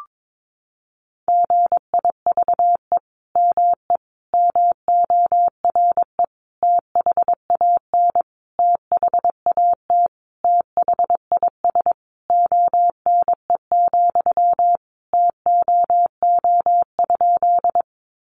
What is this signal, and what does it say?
Morse code, heard as a keyed tone, about 22 wpm, 700 Hz. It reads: GIVE ME MORE THAN THAT THIS ONE, TOO?
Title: Sentences from Top 200 Words 22wpm